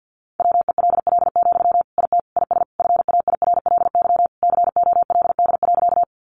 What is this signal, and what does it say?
Morse code, heard as a keyed tone, about 48 wpm, 700 Hz. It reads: WELL, IT IS FAIRLY COLD.
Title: Sentences from Top 1100 Words 50wpm